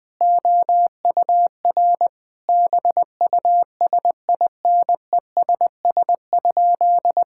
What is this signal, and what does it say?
Morse code, heard as a keyed tone, about 20 wpm, 700 Hz. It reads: OUR BUSINESS?
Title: Sentences from Top 1000 Words - Review Part 6 - 20wpm